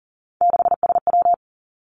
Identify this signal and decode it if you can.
Morse code, heard as a keyed tone, about 42 wpm, 700 Hz. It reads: 6SW
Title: Three Challenging Letter-Number Combinations 40wpm